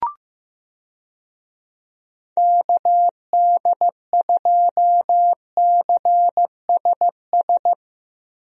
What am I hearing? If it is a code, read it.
Morse code, heard as a keyed tone, about 15 wpm, 700 Hz. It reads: KD2CSS
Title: USA Call Signs - 6 Characters - Repeated 2x - 15wpm